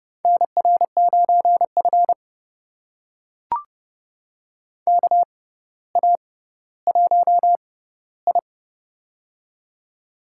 Morse code, heard as a keyed tone, about 30 wpm, spaced as for 15 wpm, 700 Hz. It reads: NR9F KA1I
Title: USA Call Signs - 4 Characters - 6x Character Spacing - 30wpm